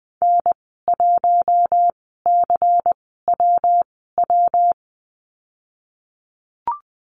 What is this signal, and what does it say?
Morse code, heard as a keyed tone, about 20 wpm, spaced as for 15 wpm, 700 Hz. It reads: N1CWW
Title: 140 Lesson 23g - Letter f - Call Signs 20wpm